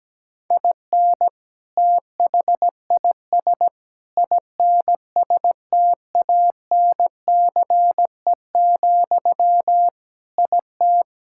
Morse code, heard as a keyed tone, about 17 wpm, 700 Hz. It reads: IN THIS INSTANCE, IT
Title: Sentences from Top 1000 Words - Review Part 5 - 17wpm